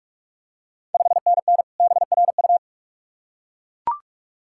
Morse code, heard as a keyed tone, about 44 wpm, 690 Hz. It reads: 5NN BRU
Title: Ontario QSO Party Exchange 45wpm